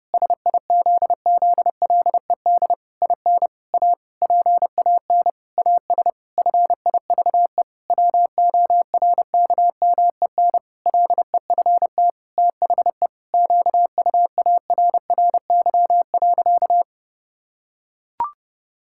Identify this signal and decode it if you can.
Morse code, heard as a keyed tone, about 30 wpm, 700 Hz. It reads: SIZZLED IN A PAN AS FIVE WORKMEN LEFT THE QUARRY.
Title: Pangram Challenge v2 - 30wpm